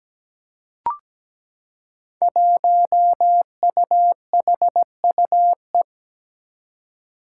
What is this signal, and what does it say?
Morse code, heard as a keyed tone, about 17 wpm, 700 Hz. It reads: 1UHUE
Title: Five Challenging Letter-Number Combinations 17wpm